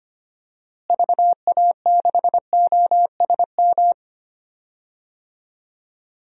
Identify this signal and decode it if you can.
Morse code, heard as a keyed tone, about 25 wpm, 700 Hz. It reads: VA6OSM
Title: DXCC-100 Call Signs - 6 Characters - Repeated 2x - 25wpm